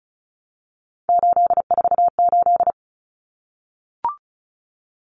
Morse code, heard as a keyed tone, about 34 wpm, 700 Hz. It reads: 848
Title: Numbers - 3-Digit 35wpm